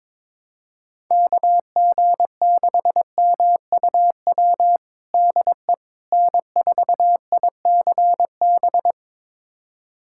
Morse code, heard as a keyed tone, about 22 wpm, 700 Hz. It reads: KG6MUW DE N4ICB
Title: USA Callsign DE Callsign 22wpm